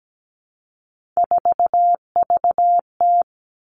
Morse code, heard as a keyed tone, about 17 wpm, 700 Hz. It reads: 4VT
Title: All Cut Numbers - 3-Digit 17wpm